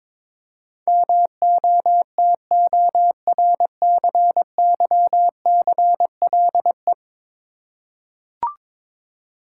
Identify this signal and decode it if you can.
Morse code, heard as a keyed tone, about 22 wpm, 700 Hz. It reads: MOTORCYCLE
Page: morse-code-ninja-practice-22wpm